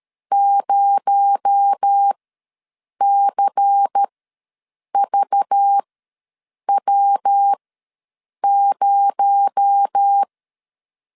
Morse code, heard as a keyed tone, about 13 wpm, spaced as for 7 wpm, 800 Hz. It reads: 0CVW0
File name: LEZIONI / 6 PKX0 / 6 PKX0 ws 12wpm 3448